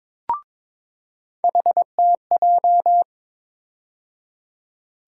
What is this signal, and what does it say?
Morse code, heard as a keyed tone, about 21 wpm, 700 Hz. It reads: HTJ
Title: USA Call Sign Postfixes - 3 Letters 22wpm